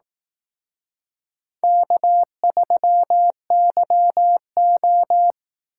Morse code, heard as a keyed tone, about 17 wpm, 700 Hz. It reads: K3YO
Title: Speed Racing - USA 1x2 Call Signs 15wpm